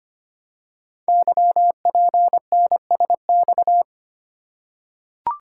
Morse code, heard as a keyed tone, about 25 wpm, 700 Hz. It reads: YPNSX
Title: Code Groups - 5 Letter:Numbers - 25wpm